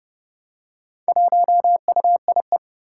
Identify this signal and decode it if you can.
Morse code, heard as a keyed tone, about 30 wpm, 700 Hz. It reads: 1UIE
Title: Four Challenging Letter-Number Combinations 30wpm